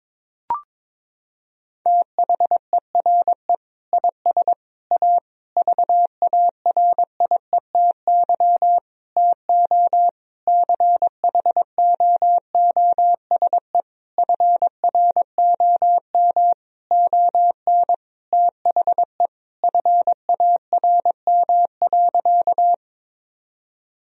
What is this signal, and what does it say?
Morse code, heard as a keyed tone, about 22 wpm, 700 Hz. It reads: THERE IS A VARIETY TO CHOOSE FROM ON THE FARM.
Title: Sentences from Top 1000 Words - Review Part 4 - 22wpm